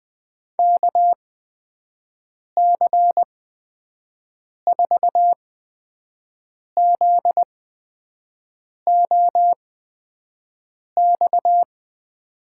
Morse code, heard as a keyed tone, about 20 wpm, spaced as for 5 wpm, 700 Hz. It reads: KC4ZOX
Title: USA Call Signs - 6 Characters - 8x Character Spacing - 20wpm